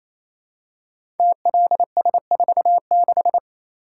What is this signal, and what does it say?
Morse code, heard as a keyed tone, about 27 wpm, 700 Hz. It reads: TLS46
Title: Code Groups - 5 Letter:Numbers - 28wpm